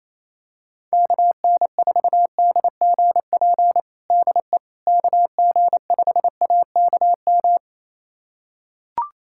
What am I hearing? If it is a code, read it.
Morse code, heard as a keyed tone, about 28 wpm, 700 Hz. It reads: KN4DGP DE KG5AKM
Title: USA Callsign DE Callsign - 6 Characters - 28wpm